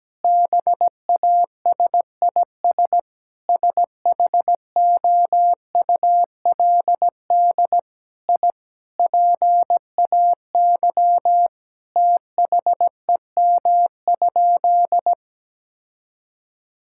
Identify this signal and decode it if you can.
Morse code, heard as a keyed tone, about 17 wpm, 700 Hz. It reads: BASIS SHOULD I PAY THEM?
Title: Sentences from Top 1000 Words - Review Part 5 - 17wpm